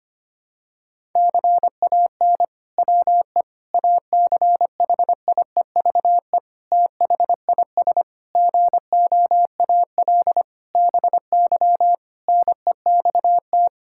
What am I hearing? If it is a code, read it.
Morse code, heard as a keyed tone, about 25 wpm, 700 Hz. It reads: CAN WE ACHIEVE THIS GOAL BY NEXT
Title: Sentences from Top 1000 Words - Review Part 5 - 25wpm